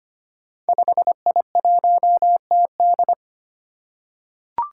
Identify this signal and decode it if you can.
Morse code, heard as a keyed tone, about 25 wpm, 700 Hz. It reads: 5I1TD
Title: Five Challenging Letter-Number Combinations 25wpm